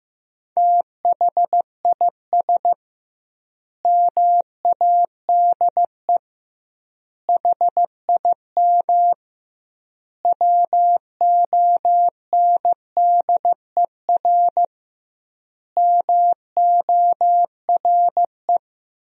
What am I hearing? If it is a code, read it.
Morse code, heard as a keyed tone, about 15 wpm, 700 Hz. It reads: THIS MADE HIM WONDER MORE
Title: 2x Word-Spacing - Sentences from Top 500 Words - 15wpm